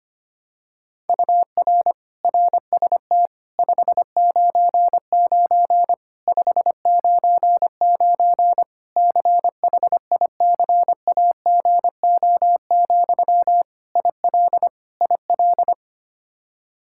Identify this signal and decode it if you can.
Morse code, heard as a keyed tone, about 25 wpm, 700 Hz. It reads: UR RST 599 599 CHICAGO, IL IL
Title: Practice QSO - Exchange 1 - Beginning - 25wpm